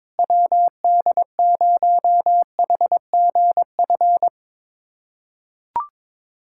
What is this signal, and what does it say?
Morse code, heard as a keyed tone, about 22 wpm, 700 Hz. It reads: WD0HGF
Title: USA Callsign DE Callsign - 6 Characters - 22wpm